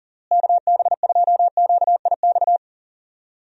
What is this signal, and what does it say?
Morse code, heard as a keyed tone, about 40 wpm, 700 Hz. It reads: KB2QIX
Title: USA Call Signs - 6 Characters - Repeated 3x - 40wpm